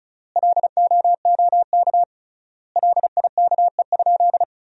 Morse code, heard as a keyed tone, about 36 wpm, 700 Hz. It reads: LOOK LIKE?
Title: 3x Word-Spacing - Sentences from Top 100 Words - 35wpm